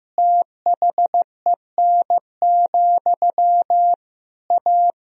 Morse code, heard as a keyed tone, about 15 wpm, 700 Hz. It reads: THEN, A
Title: Sentences from Top 1000 Words - Review Part 3 - 15wpm